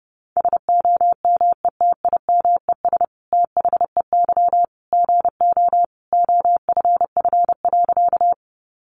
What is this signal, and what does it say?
Morse code, heard as a keyed tone, about 30 wpm, 700 Hz. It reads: SOMETIMES THEY GO OFF.
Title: Sentences from Top 600 Words 30wpm